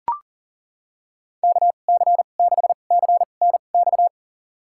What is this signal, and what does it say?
Morse code, heard as a keyed tone, about 38 wpm, spaced as for 28 wpm, 700 Hz. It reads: KC6CNX
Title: USA Call Signs - 6 Characters - 2x Character Spacing - 40wpm